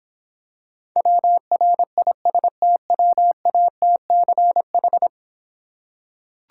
Morse code, heard as a keyed tone, about 26 wpm, 700 Hz. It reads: WRISTWATCH